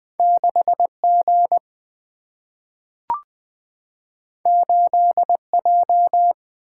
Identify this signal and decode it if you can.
Morse code, heard as a keyed tone, about 20 wpm, 700 Hz. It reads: 6G 8J
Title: Single-Letter Single-Number 20wpm